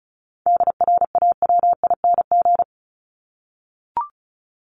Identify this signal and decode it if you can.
Morse code, heard as a keyed tone, about 34 wpm, 700 Hz. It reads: DRAWING
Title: ING Suffix - Top 100 Words with Base Words 35wpm